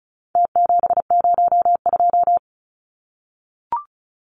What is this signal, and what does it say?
Morse code, heard as a keyed tone, about 34 wpm, 700 Hz. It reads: T702
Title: Numbers - Contest Style - 4-Digit 35wpm